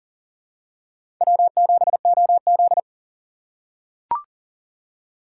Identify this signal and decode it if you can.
Morse code, heard as a keyed tone, about 38 wpm, 700 Hz. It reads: W7OZ
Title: USA Call Signs - 2x Character Spacing - 40wpm